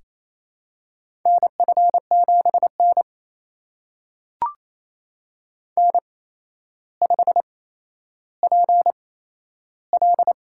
Morse code, heard as a keyed tone, about 28 wpm, spaced as for 12 wpm, 700 Hz. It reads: NF7N N5PL
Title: USA Call Signs - 4 Characters - 8x Character Spacing - 28wpm